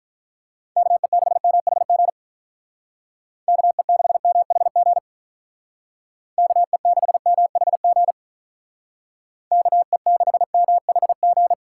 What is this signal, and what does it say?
Morse code, heard as a keyed tone, about 43 wpm, 700 Hz. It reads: KE6MHG KE6MHG KE6MHG KE6MHG
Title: Speed Racing - USA 2x3 Call Signs 35wpm